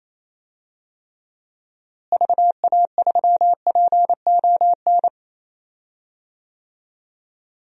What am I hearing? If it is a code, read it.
Morse code, heard as a keyed tone, about 28 wpm, 700 Hz. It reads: VA3PON
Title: DXCC-100 Call Signs - 6 Characters - Repeated 3x - 28wpm